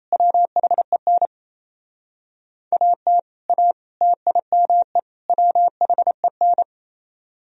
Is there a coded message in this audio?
Yes: Morse code from 0.1 to 6.6 s, heard as a keyed tone, about 36 wpm, 700 Hz. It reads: WHEN AT A TIME WHEN